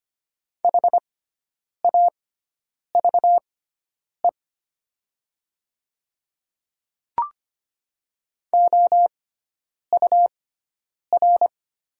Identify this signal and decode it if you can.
Morse code, heard as a keyed tone, about 25 wpm, spaced as for 9 wpm, 700 Hz. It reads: HAVE OUR
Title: 6x Character Spacing - Top 200 Words - 25wpm